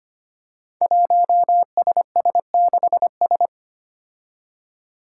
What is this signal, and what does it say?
Morse code, heard as a keyed tone, about 25 wpm, 700 Hz. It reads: 1SS6S
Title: Five Challenging Letter-Number Combinations 25wpm